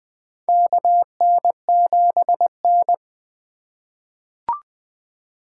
Morse code, heard as a keyed tone, about 20 wpm, 700 Hz. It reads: KN7N
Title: USA Call Signs - 8x Character Spacing - 20wpm